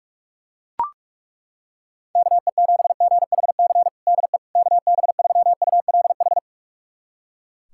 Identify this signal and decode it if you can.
Morse code, heard as a keyed tone, about 42 wpm, 700 Hz. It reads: KE7GHC DE KB3ULH
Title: USA Callsign DE Callsign - 6 Characters - 45wpm